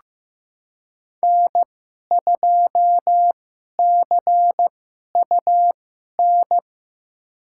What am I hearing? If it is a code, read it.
Morse code, heard as a keyed tone, about 15 wpm, spaced as for 11 wpm, 700 Hz. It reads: N2CUN
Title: USA Call Signs - 2x Character Spacing - 15wpm